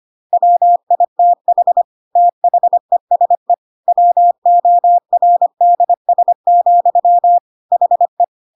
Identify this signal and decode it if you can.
Morse code, heard as a keyed tone, about 25 wpm, 700 Hz. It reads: WITH THESE WORDS, HE
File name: Pinocchio / Pinocchio ch1 25 25 ik1hge